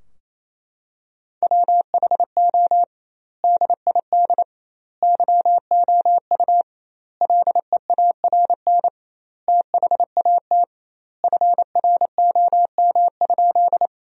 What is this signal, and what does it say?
Morse code, heard as a keyed tone, about 28 wpm, 700 Hz. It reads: WHO DID YOU LEARN THAT FROM?
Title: 2x Word-Spacing - Sentences from Top 300 Words - 28wpm